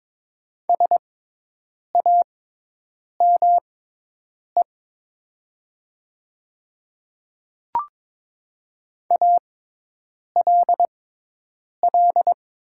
Morse code, heard as a keyed tone, about 22 wpm, spaced as for 8 wpm, 700 Hz. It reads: SAME ALL
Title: 6x Character Spacing - Top 300 Words - 22wpm